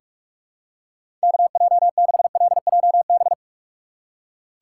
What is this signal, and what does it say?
Morse code, heard as a keyed tone, about 47 wpm, 700 Hz. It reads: KJ6LJB
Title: USA Call Signs - 6 Characters - 6x Character Spacing - 45wpm